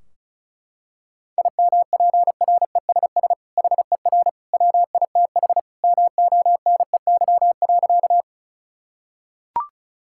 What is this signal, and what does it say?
Morse code, heard as a keyed tone, about 36 wpm, 700 Hz. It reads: IMPRESS HER WITH MONEY.
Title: Sentences from Top 1600 Words 35wpm